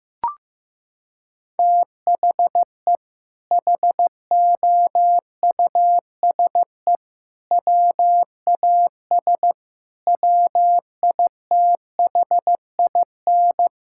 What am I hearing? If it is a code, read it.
Morse code, heard as a keyed tone, about 15 wpm, 700 Hz. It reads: THE HOUSE WAS WITHIN
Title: Sentences from Top 1000 Words - Review Part 6 - 15wpm